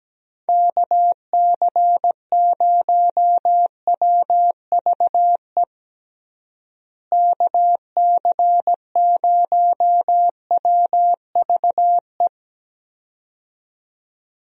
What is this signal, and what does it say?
Morse code, heard as a keyed tone, about 17 wpm, 700 Hz. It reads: KC0WVE KC0WVE
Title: USA Call Signs - 6 Characters - Repeated 2x - 17wpm